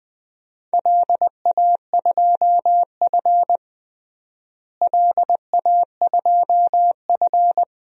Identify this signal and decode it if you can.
Morse code, heard as a keyed tone, about 20 wpm, 700 Hz. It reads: LA2F LA2F
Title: DXCC-100 Call Signs - 4 Characters - Repeated 2x - 20wpm